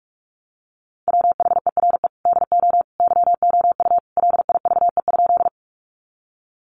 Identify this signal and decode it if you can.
Morse code, heard as a keyed tone, about 44 wpm, 700 Hz. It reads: WHERE DO YOU LIVE?
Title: Sentences from Top 200 Words 45wpm